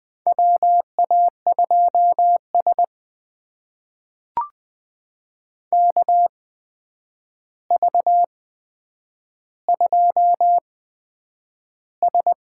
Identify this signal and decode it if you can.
Morse code, heard as a keyed tone, about 20 wpm, spaced as for 9 wpm, 700 Hz. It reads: WA2S KV2S
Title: USA Call Signs - 4 Characters - 8x Character Spacing - 20wpm